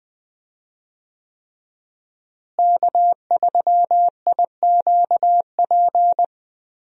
Morse code, heard as a keyed tone, about 20 wpm, 700 Hz. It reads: K3IQP
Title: USA Call Signs - 5 Characters - Repeated 3x - 20wpm